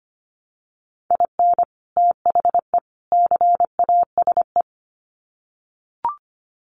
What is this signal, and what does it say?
Morse code, heard as a keyed tone, about 25 wpm, 700 Hz. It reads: IN THE CASE
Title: Sets of 3 Words 25wpm